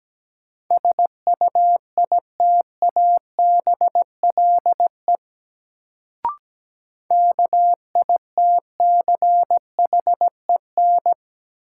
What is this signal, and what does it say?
Morse code, heard as a keyed tone, about 17 wpm, 700 Hz. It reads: SUITABLE KITCHEN
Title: Sentences from Top 1400 Words - New Word Drill - 17wpm